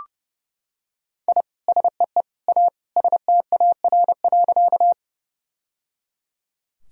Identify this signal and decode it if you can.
Morse code, heard as a keyed tone, about 30 wpm, 700 Hz. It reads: I SEE A STAR.
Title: Sentences from Top 1000 Words - Review Part 2 - 30wpm